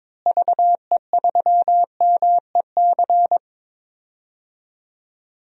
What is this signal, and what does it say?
Morse code, heard as a keyed tone, about 23 wpm, 700 Hz. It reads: VE3MEC
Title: DXCC-100 Call Signs - 6 Characters - Repeated 3x - 22wpm